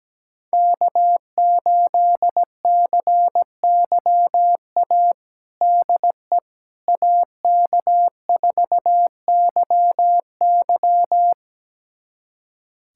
Morse code, heard as a keyed tone, about 17 wpm, 700 Hz. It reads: K8CYA DE AK4YY
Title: USA Callsign DE Callsign - 5 Characters - 17wpm